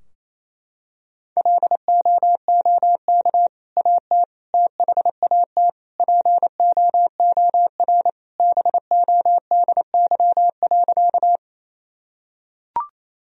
Morse code, heard as a keyed tone, about 27 wpm, 700 Hz. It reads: LOOK AT THAT POOR BODY.
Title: Sentences from Top 1000 Words - Review Part 2 - 28wpm